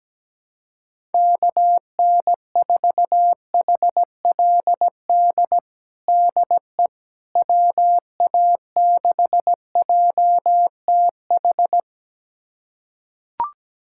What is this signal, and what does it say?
Morse code, heard as a keyed tone, about 17 wpm, 700 Hz. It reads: KN4HLD DE WA6JTH
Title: USA Callsign DE Callsign 17wpm